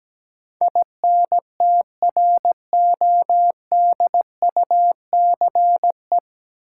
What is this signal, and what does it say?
Morse code, heard as a keyed tone, about 17 wpm, 700 Hz. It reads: INTRODUCE
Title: Sentences from Top 1600 Words - New Word Drill Alt - 17wpm